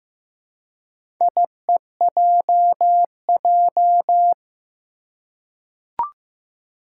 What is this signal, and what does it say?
Morse code, heard as a keyed tone, about 15 wpm, 700 Hz. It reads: IEJJ
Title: Four Challenging Letter-Number Combinations 15wpm